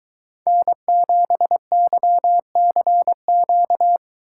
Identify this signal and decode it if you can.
Morse code, heard as a keyed tone, about 22 wpm, 700 Hz. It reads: N7YCQ